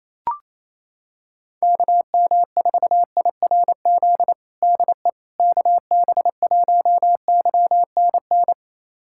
Morse code, heard as a keyed tone, about 28 wpm, 700 Hz. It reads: KM4IRZ DE KB1YNN
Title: USA Callsign DE Callsign - 6 Characters - 28wpm